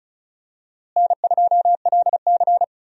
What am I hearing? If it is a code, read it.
Morse code, heard as a keyed tone, about 36 wpm, 700 Hz. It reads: N2LC
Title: USA Call Signs - 4 Characters - Repeated 3x - 35wpm